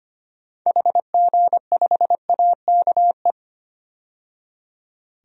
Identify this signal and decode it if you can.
Morse code, heard as a keyed tone, about 25 wpm, 700 Hz. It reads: HG5AKE
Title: DXCC-100 Call Signs - 6 Characters - 25wpm